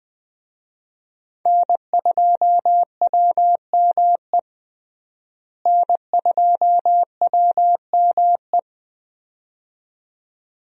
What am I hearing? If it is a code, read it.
Morse code, heard as a keyed tone, about 20 wpm, 700 Hz. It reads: N2WME N2WME
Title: USA Call Signs - 5 Characters - Repeated 2x - 20wpm